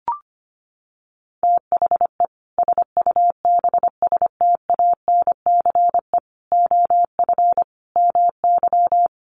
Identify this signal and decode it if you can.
Morse code, heard as a keyed tone, about 25 wpm, 700 Hz. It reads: THE SUBSTANCE OF MY
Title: Sentences from Top 1500 Words 25wpm